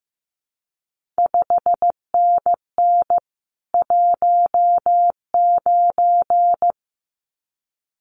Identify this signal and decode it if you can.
Morse code, heard as a keyed tone, about 15 wpm, 700 Hz. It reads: 5NN 19
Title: CQ WW DX Contest Exchange Encore - 15wpm